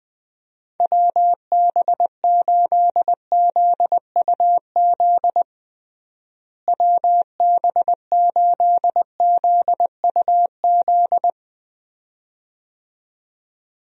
Morse code, heard as a keyed tone, about 20 wpm, 700 Hz. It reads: WB8ZUZ WB8ZUZ
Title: USA Call Signs - 6 Characters - Repeated 2x - 20wpm